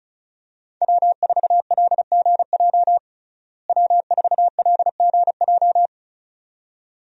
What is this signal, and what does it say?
Morse code, heard as a keyed tone, about 36 wpm, 700 Hz. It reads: W4LGJ W4LGJ
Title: USA Call Signs - 5 Characters - Repeated 2x - 35wpm